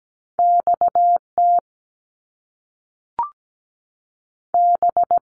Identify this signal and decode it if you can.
Morse code, heard as a keyed tone, about 17 wpm, 700 Hz. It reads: X T B
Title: Top Two Letter Combinations in Words 17wpm